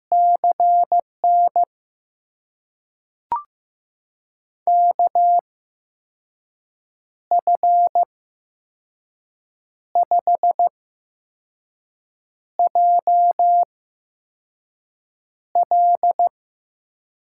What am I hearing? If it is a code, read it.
Morse code, heard as a keyed tone, about 15 wpm, spaced as for 5 wpm, 700 Hz. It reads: CN KF5JL
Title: USA Call Signs - 5 Characters - 8x Character Spacing - 15wpm